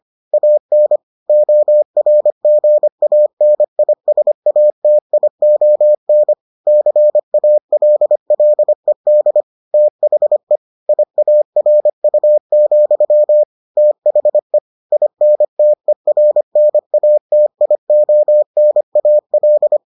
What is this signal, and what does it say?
Morse code, heard as a keyed tone, about 25 wpm, 600 Hz. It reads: AN ORGANISATION CALLED THE IARU, THE INTERNATIONAL